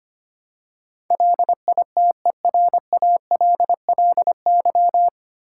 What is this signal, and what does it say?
Morse code, heard as a keyed tone, about 25 wpm, 700 Hz. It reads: LITERALLY
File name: practice / 25wpm / 9 Letter Words - First 3 Letters Given - 25wpm